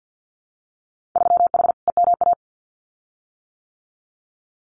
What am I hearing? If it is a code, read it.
Morse code, heard as a keyed tone, about 50 wpm, 700 Hz. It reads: 3H EMA
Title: Winter Field Day Exchanges Encore 50wpm